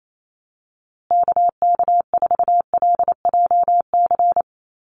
Morse code, heard as a keyed tone, about 28 wpm, 700 Hz. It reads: KK4LJC
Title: USA Call Signs Repeated 3x 28wpm